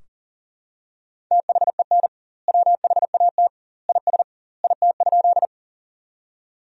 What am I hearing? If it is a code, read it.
Morse code, heard as a keyed tone, about 37 wpm, 700 Hz. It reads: THEN WHAT IS IT?
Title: 2x Word-Spacing - Sentences from Top 100 Words - 40wpm